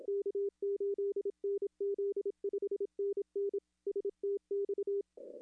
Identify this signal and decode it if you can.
Morse code, heard as a keyed tone, about 27 wpm, 410 Hz. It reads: K8NZ 5NN STX
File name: contests / arrl-160-k5tr-op-w5tn -0300